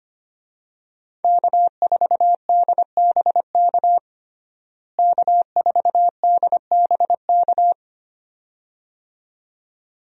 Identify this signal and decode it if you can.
Morse code, heard as a keyed tone, about 25 wpm, 700 Hz. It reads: K4DBK K4DBK
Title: USA Call Signs - 5 Characters - Repeated 2x - 25wpm